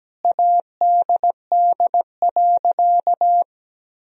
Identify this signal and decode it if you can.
Morse code, heard as a keyed tone, about 17 wpm, 700 Hz. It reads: ADD.
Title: Sentences from Top 1000 Words - Review Part 1 - 17wpm